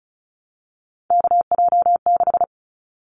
Morse code, heard as a keyed tone, about 36 wpm, 700 Hz. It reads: KJ6
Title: Callsign Prefixes for Top 25 DXCC - 35wpm